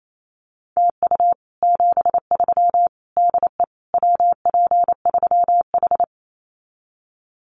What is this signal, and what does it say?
Morse code, heard as a keyed tone, about 28 wpm, 700 Hz. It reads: TU 73 DE WP3H
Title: Common QSO Phrases Encore 28wpm